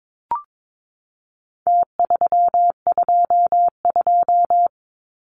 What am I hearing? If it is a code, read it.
Morse code, heard as a keyed tone, about 22 wpm, 700 Hz. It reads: T322
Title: Numbers - Contest Style - 4-Digit 22wpm